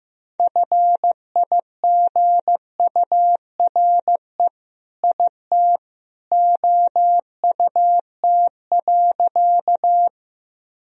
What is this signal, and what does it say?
Morse code, heard as a keyed tone, about 15 wpm, 700 Hz. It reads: FIGURE IT OUT.
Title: Sentences from Top 1000 Words - Review Part 2 - 15wpm